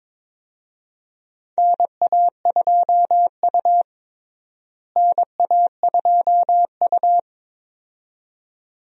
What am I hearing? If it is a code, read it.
Morse code, heard as a keyed tone, about 22 wpm, 700 Hz. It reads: NA2U NA2U
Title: USA Call Signs - 4 Characters - Repeated 2x - 22wpm